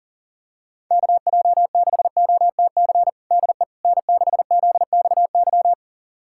Morse code, heard as a keyed tone, about 40 wpm, 700 Hz. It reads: KJ6OTC DE N6ZXY
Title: USA Callsign DE Callsign 40wpm